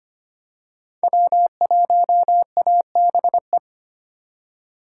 Morse code, heard as a keyed tone, about 25 wpm, 700 Hz. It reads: W1ABE
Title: USA Call Signs - 4x Character Spacing - 25wpm